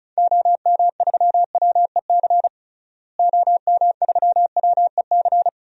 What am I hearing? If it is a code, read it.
Morse code, heard as a keyed tone, about 36 wpm, 700 Hz. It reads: OM3WEC OM3WEC
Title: DXCC-100 Call Signs - 6 Characters - Repeated 2x - 35wpm